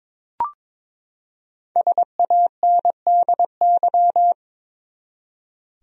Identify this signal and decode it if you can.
Morse code, heard as a keyed tone, about 22 wpm, 700 Hz. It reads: SANDY
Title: USA Common Names - Max 5 Letters - 22wpm